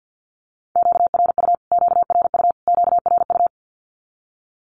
Morse code, heard as a keyed tone, about 50 wpm, 700 Hz. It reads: QRU QRU QRU
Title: Q-Codes Repeated 3x 50wpm